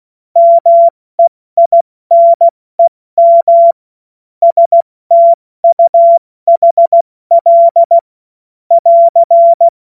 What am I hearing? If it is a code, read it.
Morse code, heard as a keyed tone, about 16 wpm, 680 Hz. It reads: MEINEM STUHL <AR>
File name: Morsekurs / Witze / Witz 09. Februar 2021, 70 F80